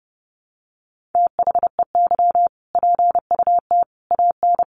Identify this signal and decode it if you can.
Morse code, heard as a keyed tone, about 30 wpm, 700 Hz. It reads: THEY PUT AN
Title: Sentences from Top 1400 Words 30wpm